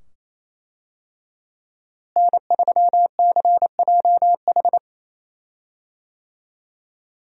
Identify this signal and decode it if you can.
Morse code, heard as a keyed tone, about 28 wpm, 700 Hz. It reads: N3CJH
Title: USA Call Signs - 5 Characters - Repeated 2x - 28wpm